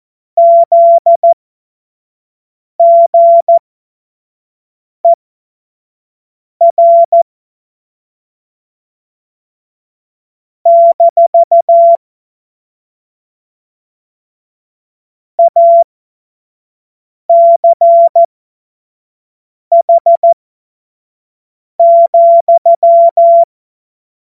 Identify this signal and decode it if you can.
Morse code, heard as a keyed tone, about 14 wpm, spaced as for 5 wpm, 680 Hz. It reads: ZGER - ACH,
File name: Morsekurs/Witze/Witz 09 Januar 2024 25 F70